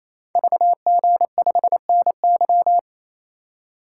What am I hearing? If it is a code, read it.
Morse code, heard as a keyed tone, about 28 wpm, 700 Hz. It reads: VG5NY
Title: Code Groups - 5 Letter:Numbers - 28wpm